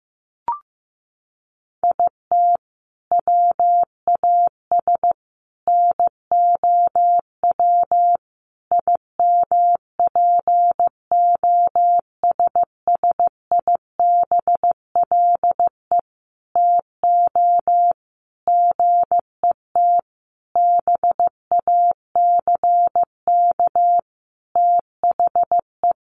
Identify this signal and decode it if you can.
Morse code, heard as a keyed tone, about 15 wpm, 700 Hz. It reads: IT WAS NOW IMPOSSIBLE TO GET BACK THE
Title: Sentences from Top 1100 Words 15wpm